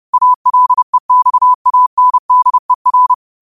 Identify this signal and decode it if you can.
Morse code, heard as a keyed tone, about 30 wpm, 1000 Hz. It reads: ALEXANDER
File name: qotdfiles / 30wpm